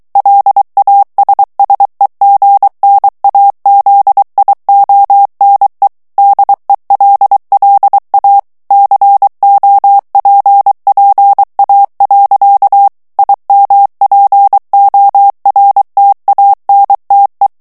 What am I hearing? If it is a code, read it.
Morse code, heard as a keyed tone, about 23 wpm, 800 Hz. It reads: LASSEGNAZIONE DELLA COPPA. IMPORTANTE
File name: ITALIEN texte clair 2 vitesse 22 mots minute